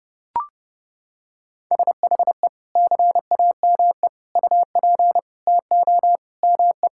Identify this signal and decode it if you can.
Morse code, heard as a keyed tone, about 30 wpm, 700 Hz. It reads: SHE CAME UP TO ME
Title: Sentences from Top 1600 Words 30wpm